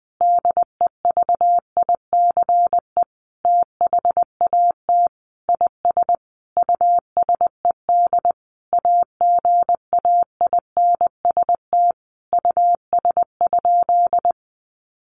Morse code, heard as a keyed tone, about 20 wpm, 700 Hz. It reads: DEVICE THAT IS USED AGAINST US?
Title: Sentences from Top 1500 Words 20wpm